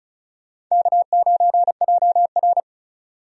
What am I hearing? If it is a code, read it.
Morse code, heard as a keyed tone, about 36 wpm, 700 Hz. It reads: K9JR
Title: USA Call Signs - 4 Characters - 35wpm